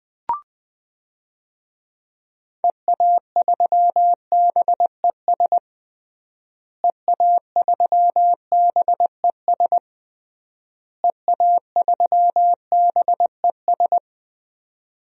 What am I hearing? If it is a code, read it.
Morse code, heard as a keyed tone, about 20 wpm, 700 Hz. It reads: EA3BES EA3BES EA3BES
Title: DXCC-100 Call Signs - 6 Characters - Repeated 3x - 20wpm